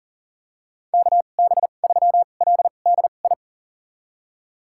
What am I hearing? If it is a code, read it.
Morse code, heard as a keyed tone, about 40 wpm, spaced as for 29 wpm, 700 Hz. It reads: KB3LDI
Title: USA Call Signs - 2x Character Spacing - 40wpm